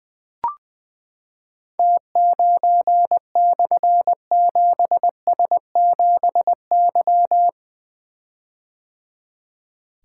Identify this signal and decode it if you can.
Morse code, heard as a keyed tone, about 20 wpm, 700 Hz. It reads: T9/7S7Y
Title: Difficult Call Signs 20wpm